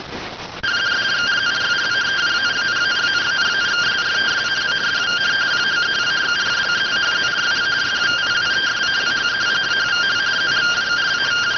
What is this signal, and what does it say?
Two-tone teleprinter signal, about 170 Hz shift.